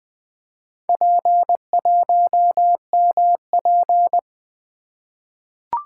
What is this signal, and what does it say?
Morse code, heard as a keyed tone, about 20 wpm, 700 Hz. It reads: P1MP
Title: Four Letter-Number Combinations - 20wpm